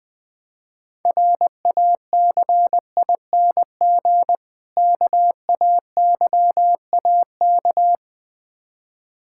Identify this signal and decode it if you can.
Morse code, heard as a keyed tone, about 20 wpm, 690 Hz. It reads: RACING KAYAK
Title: Difficult Words - Speed-Racing with Hints - 20wpm